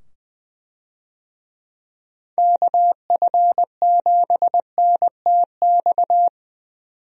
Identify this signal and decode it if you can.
Morse code, heard as a keyed tone, about 21 wpm, 700 Hz. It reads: KF7NTX
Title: USA Call Signs - 6 Characters - 20wpm